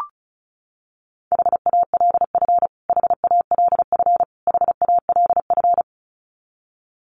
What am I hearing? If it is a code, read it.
Morse code, heard as a keyed tone, about 34 wpm, 700 Hz. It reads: HALF HALF HALF
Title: Top 500 Words Repeated 35wpm